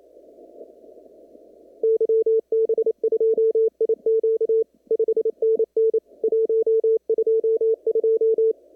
Morse code, heard as a keyed tone, about 27 wpm, 450 Hz. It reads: YB2IQ 5NN 122
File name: CW / kt5j wpx cw-op-k5pi